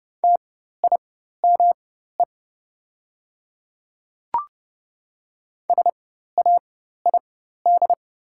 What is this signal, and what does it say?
Morse code, heard as a keyed tone, about 30 wpm, spaced as for 14 wpm, 690 Hz. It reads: TIME SAID
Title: 4x Character Spacing - Top 100 Words - 30wpm